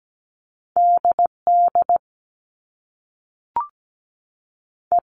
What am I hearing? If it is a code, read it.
Morse code, heard as a keyed tone, about 17 wpm, 700 Hz. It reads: DD E
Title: Top Two Letter Combinations in Words 17wpm